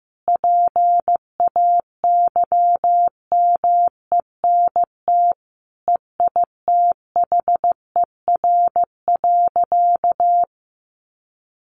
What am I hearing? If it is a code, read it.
Morse code, heard as a keyed tone, about 15 wpm, 700 Hz. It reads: PAYMENT EITHER.
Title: Sentences from Top 1200 Words 15wpm